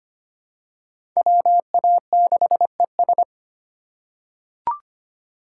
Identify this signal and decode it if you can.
Morse code, heard as a keyed tone, about 25 wpm, 700 Hz. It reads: WA6ES
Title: USA Call Signs - 5 Characters - 3x Character Spacing - 25wpm